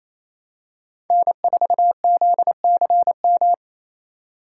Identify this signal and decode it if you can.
Morse code, heard as a keyed tone, about 28 wpm, 700 Hz. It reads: N4ZCM
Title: USA Call Signs - 5 Characters - 4x Character Spacing - 28wpm